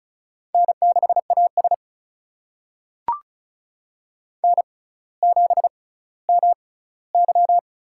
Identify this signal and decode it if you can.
Morse code, heard as a keyed tone, about 36 wpm, spaced as for 18 wpm, 700 Hz. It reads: N6AS N7MY
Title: USA Call Signs - 4 Characters - 6x Character Spacing - 35wpm